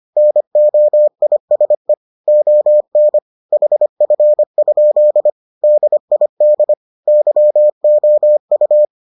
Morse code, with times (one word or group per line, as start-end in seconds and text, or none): 0.2-1.9 NOISE
2.3-3.2 ON
3.5-5.3 HF?
5.6-6.7 DID
7.1-8.9 YOU